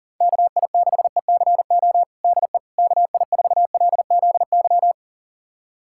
Morse code, heard as a keyed tone, about 40 wpm, 700 Hz. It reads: KI6ECO DE KI4LZY
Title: USA Callsign DE Callsign - 6 Characters - 40wpm